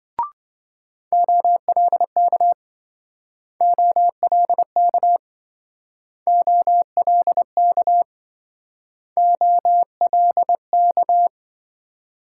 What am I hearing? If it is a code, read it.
Morse code, heard as a keyed tone, about 27 wpm, 700 Hz. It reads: OLK OLK OLK OLK